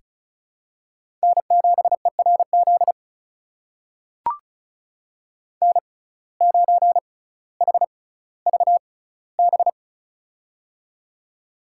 Morse code, heard as a keyed tone, about 36 wpm, spaced as for 18 wpm, 700 Hz. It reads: N7ERZ N9HVB
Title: USA Call Signs - 6x Character Spacing - 35wpm